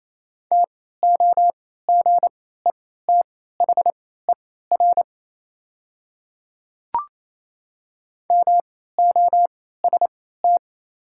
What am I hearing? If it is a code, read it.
Morse code, heard as a keyed tone, about 28 wpm, spaced as for 16 wpm, 700 Hz. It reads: TOGETHER MOST
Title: 3x Character Spacing - Top 300 Words - 28wpm